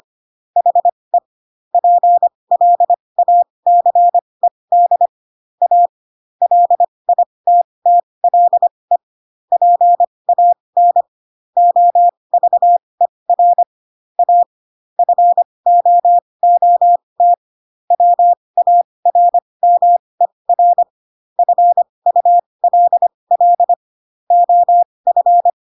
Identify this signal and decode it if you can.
Morse code, heard as a keyed tone, about 25 wpm, spaced as for 20 wpm, 700 Hz. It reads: HE PLACED A LITTLE PAN OVER A FOOT WARMER FULL OF